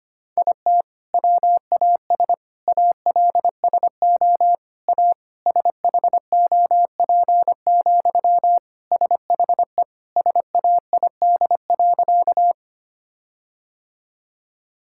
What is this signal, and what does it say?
Morse code, heard as a keyed tone, about 25 wpm, 700 Hz. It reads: IT WAS ALSO A SHOP, SHE SAID.
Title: Sentences from Top 1600 Words 25wpm